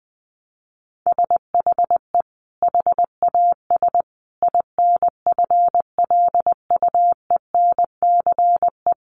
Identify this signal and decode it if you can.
Morse code, heard as a keyed tone, about 20 wpm, 700 Hz. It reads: SHE HAS INFLUENCE
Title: Sentences from Top 1100 Words 20wpm